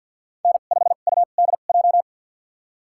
Morse code, heard as a keyed tone, about 53 wpm, spaced as for 37 wpm, 700 Hz. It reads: N5UDJ
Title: USA Call Signs - 5 Characters - 2x Character Spacing - 50wpm